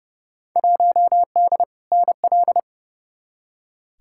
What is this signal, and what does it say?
Morse code, heard as a keyed tone, about 32 wpm, 700 Hz. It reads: 1D NL
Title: Field Day Exchanges Encore - 30wpm